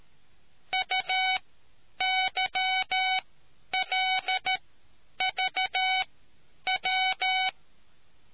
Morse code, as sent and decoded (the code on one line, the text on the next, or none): ..- -.-- .-.. ...- .--
UYLVW